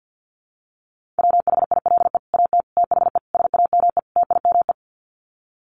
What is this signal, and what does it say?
Morse code, heard as a keyed tone, about 50 wpm, 700 Hz. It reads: WHILE AT THE SAME TIME
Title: Sets of 5 Words 50wpm